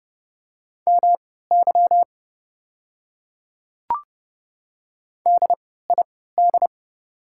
Morse code, heard as a keyed tone, about 30 wpm, spaced as for 17 wpm, 700 Hz. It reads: MY DID
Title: 3x Character Spacing - Top 100 Words - 30wpm